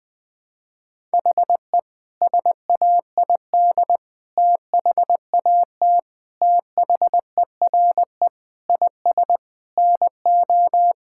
Morse code, heard as a keyed tone, about 20 wpm, 700 Hz. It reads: HE SAID THAT THERE IS NO
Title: Sentences with 4-Word Phrases - 20wpm